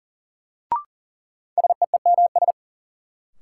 Morse code, heard as a keyed tone, about 40 wpm, 700 Hz. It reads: SEEMS
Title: Sets of 3 Words - New Word Drill - 40wpm